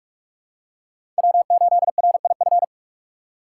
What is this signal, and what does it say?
Morse code, heard as a keyed tone, about 46 wpm, 700 Hz. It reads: W8RIF
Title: USA Call Signs - 5 Characters - 3x Character Spacing - 45wpm